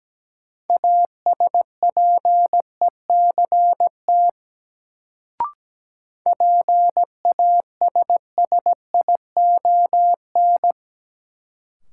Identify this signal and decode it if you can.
Morse code, heard as a keyed tone, about 17 wpm, 700 Hz. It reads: ASPECT PASSION
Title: Sentences from Top 1300 Words - New Word Drill - 17wpm